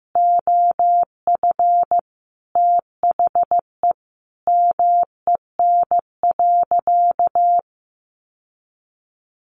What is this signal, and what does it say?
Morse code, heard as a keyed tone, about 15 wpm, 700 Hz. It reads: OF THE MEN.
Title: Sentences from Top 1400 Words 15wpm